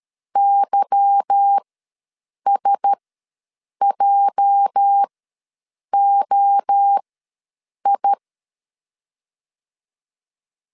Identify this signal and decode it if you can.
Morse code, heard as a keyed tone, about 13 wpm, spaced as for 7 wpm, 800 Hz. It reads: YSJOI
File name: LEZIONI / 8 YJ45 / 8 YJ ws 12wpm 3342